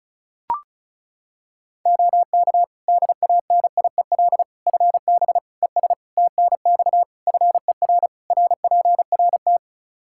Morse code, heard as a keyed tone, about 36 wpm, 700 Hz. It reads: OK DANIEL FB ES TNX FER RPRT
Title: Practice QSO - Exchange 2 - Info - 35wpm